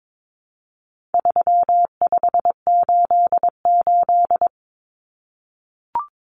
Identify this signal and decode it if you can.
Morse code, heard as a keyed tone, about 22 wpm, 700 Hz. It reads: 3588
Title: Common Cut Numbers - 4-Digit 22wpm